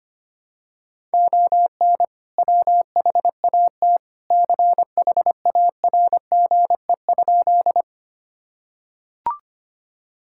Morse code, heard as a keyed tone, about 25 wpm, 700 Hz. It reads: ON WHAT CHARGE?
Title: Sentences from Top 1000 Words - Review Part 6 - 25wpm